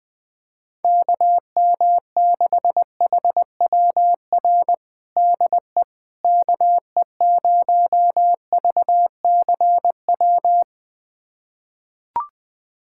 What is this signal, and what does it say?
Morse code, heard as a keyed tone, about 21 wpm, 700 Hz. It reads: KM6HWR DE KE0VCW
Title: USA Callsign DE Callsign - 6 Characters - 20wpm